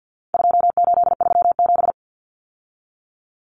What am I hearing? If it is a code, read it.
Morse code, heard as a keyed tone, about 50 wpm, 700 Hz. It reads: 2837
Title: Common Cut Numbers - 4-Digit 50wpm